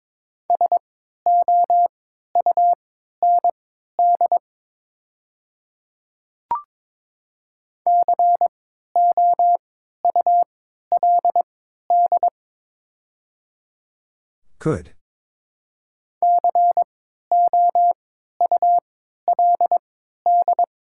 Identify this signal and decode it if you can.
Morse code, heard as a keyed tone, about 22 wpm, spaced as for 13 wpm, 700 Hz. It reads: SOUND COULD COULD
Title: 3x Character Spacing - Top 100 Words - 22wpm